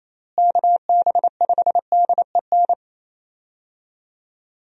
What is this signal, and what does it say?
Morse code, heard as a keyed tone, about 28 wpm, 700 Hz. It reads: KB5DEN
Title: USA Call Signs - 6 Characters - Repeated 2x - 28wpm